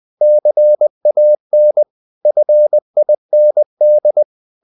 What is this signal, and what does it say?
Morse code, heard as a keyed tone, about 20 wpm, 600 Hz. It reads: CAN FIND
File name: the old man and the sea / the old man and the sea part 1 20wpm